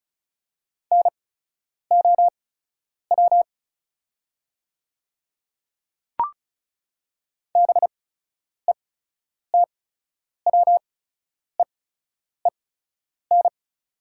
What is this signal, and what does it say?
Morse code, heard as a keyed tone, about 34 wpm, spaced as for 10 wpm, 700 Hz. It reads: NOW BETWEEN
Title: 8x Character Spacing - Top 300 Words - 35wpm